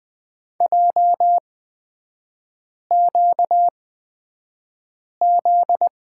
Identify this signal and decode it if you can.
Morse code, heard as a keyed tone, about 20 wpm, spaced as for 5 wpm, 700 Hz. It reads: JQZ
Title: 243 Lesson 39e - Letter z - Three Characters 20wpm